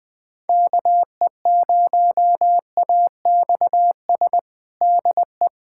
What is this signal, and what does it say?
Morse code, heard as a keyed tone, about 20 wpm, 700 Hz. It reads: KE0AXS DE
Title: USA Callsign DE Callsign 20wpm